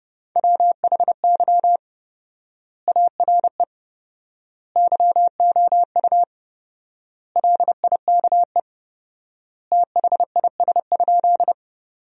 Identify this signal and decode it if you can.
Morse code, heard as a keyed tone, about 30 wpm, 700 Hz. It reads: WHY ARE YOU LIKE THIS?
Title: 4x Word-Spacing - Sentences from Top 200 Words - 30wpm